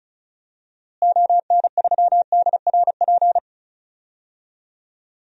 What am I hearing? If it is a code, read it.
Morse code, heard as a keyed tone, about 36 wpm, 700 Hz. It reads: ON3DRP
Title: DXCC-100 Call Signs - 6 Characters - Repeated 2x - 35wpm